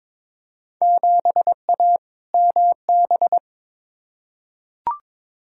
Morse code, heard as a keyed tone, about 22 wpm, 700 Hz. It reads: 7A MB
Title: Field Day Exchanges Encore - 22wpm